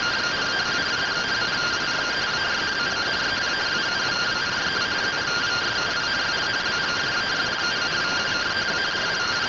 Two-tone teleprinter signal, about 170 Hz shift.